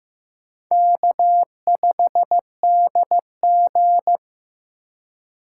Morse code, heard as a keyed tone, about 15 wpm, 700 Hz. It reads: K5DG
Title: USA Call Signs - 2x Character Spacing - 15wpm